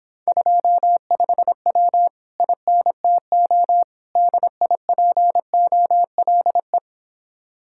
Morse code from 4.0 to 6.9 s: DIPOLE

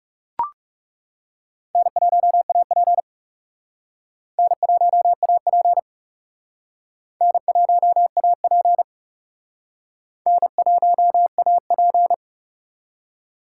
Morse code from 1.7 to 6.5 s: N1AP N1AP

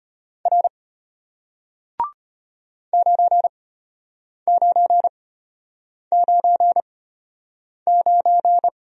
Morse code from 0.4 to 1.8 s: R

